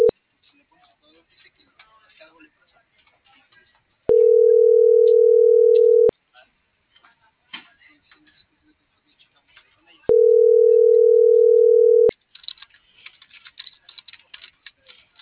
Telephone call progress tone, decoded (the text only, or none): ringback tone